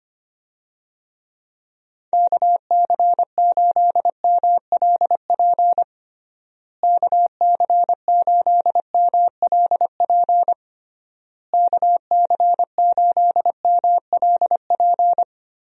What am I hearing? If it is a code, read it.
Morse code, heard as a keyed tone, about 25 wpm, 700 Hz. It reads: KC8MLP KC8MLP KC8MLP
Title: USA Call Signs - 6 Characters - Repeated 3x - 25wpm